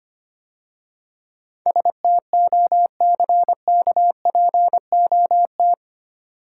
Morse code, heard as a keyed tone, about 25 wpm, 700 Hz. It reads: STOCKPOT
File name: Easy Words - Speed-Racing with Hints - 25wpm